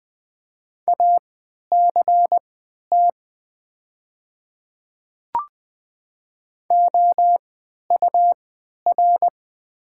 Morse code, heard as a keyed tone, about 20 wpm, spaced as for 11 wpm, 700 Hz. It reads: ACT OUR
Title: 3x Character Spacing - Top 300 Words - 20wpm